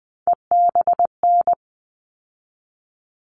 Morse code, heard as a keyed tone, about 20 wpm, 700 Hz. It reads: EBN
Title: All Cut Numbers - 3-Digit 20wpm